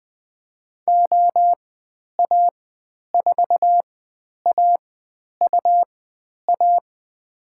Morse code, heard as a keyed tone, about 20 wpm, spaced as for 10 wpm, 700 Hz. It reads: OA4AUA
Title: 109 Lesson 18g - Letter m - Call Signs 20wpm